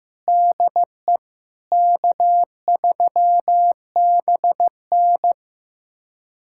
Morse code, heard as a keyed tone, about 15 wpm, 700 Hz. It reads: DE K3BN
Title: USA Callsign DE Callsign - 4 Characters - 15wpm